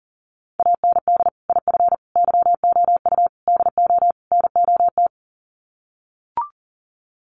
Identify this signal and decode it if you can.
Morse code, heard as a keyed tone, about 40 wpm, 700 Hz. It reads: AND IF YOU DO NOT
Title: Sets of 5 Words 40wpm